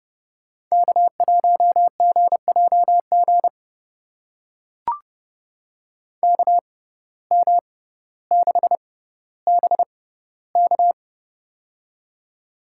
Morse code, heard as a keyed tone, about 30 wpm, spaced as for 15 wpm, 700 Hz. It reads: K1GJG KM6BK
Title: USA Call Signs - 5 Characters - 6x Character Spacing - 30wpm